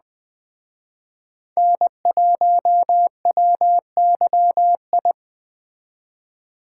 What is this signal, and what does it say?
Morse code, heard as a keyed tone, about 20 wpm, 700 Hz. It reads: N1WYI
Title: Speed Racing - USA Call Signs 17wpm